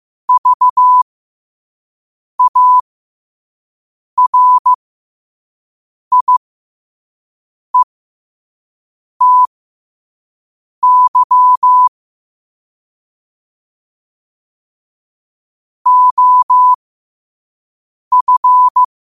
Morse code, heard as a keyed tone, about 15 wpm, spaced as for 5 wpm, 1000 Hz. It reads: VARIETY OF
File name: qotdfiles / 5wpm